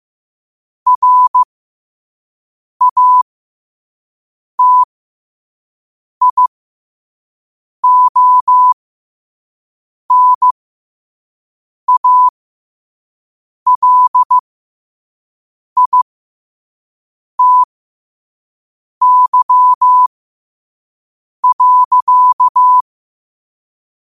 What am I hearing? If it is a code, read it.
Morse code, heard as a keyed tone, about 15 wpm, spaced as for 5 wpm, 1000 Hz. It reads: RATIONALITY.
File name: qotdfiles / 5wpm